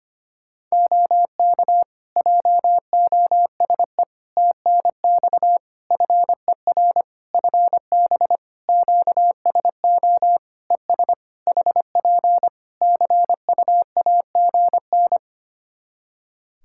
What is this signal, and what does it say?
Morse code, heard as a keyed tone, about 25 wpm, 700 Hz. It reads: OK JOSE TNX FER FB QSO ES HP CUAGN
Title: K1USN SST QSO Practice 25wpm